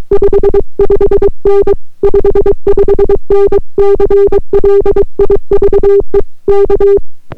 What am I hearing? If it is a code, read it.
Morse code, heard as a keyed tone, about 22 wpm, 400 Hz. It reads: 55N 55N CLIVE K